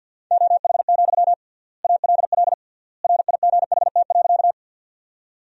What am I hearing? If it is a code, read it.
Morse code, heard as a keyed tone, about 50 wpm, 700 Hz. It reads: OH, ALL RIGHT.
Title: 3x Word-Spacing - Sentences from Top 500 Words - 50wpm